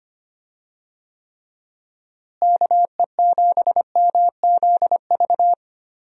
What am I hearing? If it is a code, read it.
Morse code, heard as a keyed tone, about 25 wpm, 700 Hz. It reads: KE7MZV
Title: USA Call Signs - 6 Characters - Repeated 2x - 25wpm